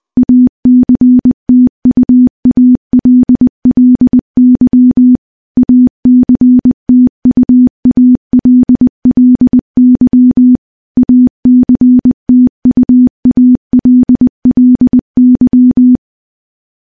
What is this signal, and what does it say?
Morse code, heard as a keyed tone, about 20 wpm, 250 Hz. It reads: ACTUALLY ACTUALLY ACTUALLY
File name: morse-training-20WPM